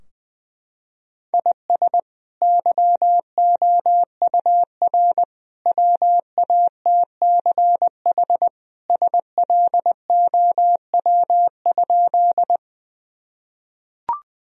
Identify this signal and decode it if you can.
Morse code, heard as a keyed tone, about 21 wpm, 700 Hz. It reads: IS YOUR WATCH SLOW?
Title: Sentences from Top 1000 Words - Review Part 2 - 20wpm